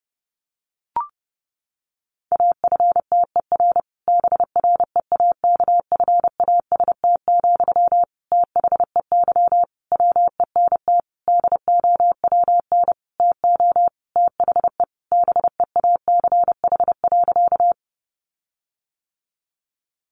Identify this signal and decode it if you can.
Morse code, heard as a keyed tone, about 30 wpm, 700 Hz. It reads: AFTER BREAKFAST, THEY WENT DOWN TO THE BEACH.
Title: Sentences from Top 1500 Words 30wpm